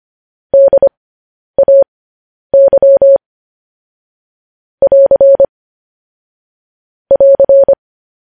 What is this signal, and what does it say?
Morse code, heard as a keyed tone, about 25 wpm, spaced as for 10 wpm, 550 Hz. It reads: DAY <AR> <AR>
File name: podcast / News headlines-1x-25x10